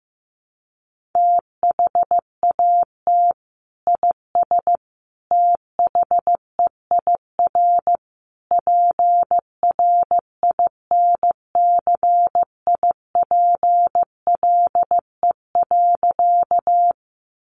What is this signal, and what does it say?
Morse code, heard as a keyed tone, about 15 wpm, 700 Hz. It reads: THAT IS THEIR PRINCIPLE.
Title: Sentences from Top 1400 Words 15wpm